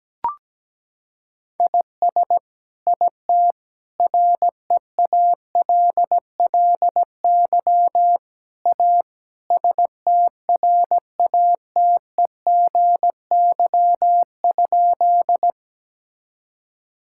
Morse code, heard as a keyed tone, about 17 wpm, 700 Hz. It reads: IS IT REALLY A STRATEGY?
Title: Sentences from Top 1000 Words - Review Part 5 - 17wpm